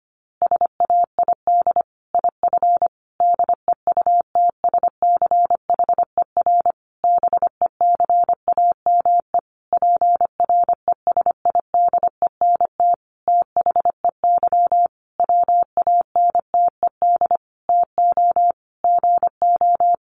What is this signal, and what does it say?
Morse code, heard as a keyed tone, about 25 wpm, 700 Hz. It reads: SAID IF DEUTSCHER BECAME PRESIDENT THEY WANTED TO GO
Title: A Sound Of Thunder 25wpm